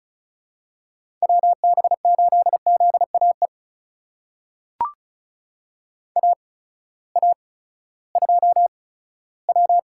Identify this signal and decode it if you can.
Morse code, heard as a keyed tone, about 36 wpm, spaced as for 18 wpm, 700 Hz. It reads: WB8ZAE AA2W
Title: USA Call Signs - 8x Character Spacing - 35wpm